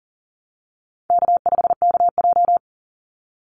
Morse code, heard as a keyed tone, about 40 wpm, 700 Hz. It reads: K5KJ
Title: Easy Call Signs Repeated 2x - 40wpm